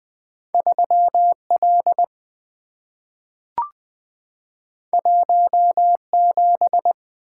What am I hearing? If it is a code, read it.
Morse code, heard as a keyed tone, about 20 wpm, 700 Hz. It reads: 3 L 1 7
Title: Two Letter-Number Combinations - 20wpm